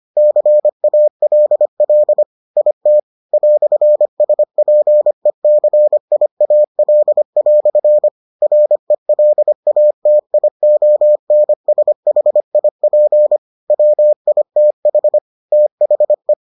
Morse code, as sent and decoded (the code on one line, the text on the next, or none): -.-. .- .-.. .-.. / .. - / .-..-. ... .--. -.-. .. .- .-.. .-..-. / .-. .-.. .- - .. --- -. ... .... .. .--. / .-- .. - .... / - .... .
CALL IT "SPECIAL" RELATIONSHIP WITH THE